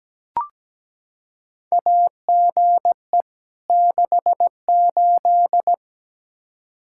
Morse code, heard as a keyed tone, about 17 wpm, 700 Hz. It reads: AGE 68
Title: Practice QSO - Exchange 3 - More Info - 17wpm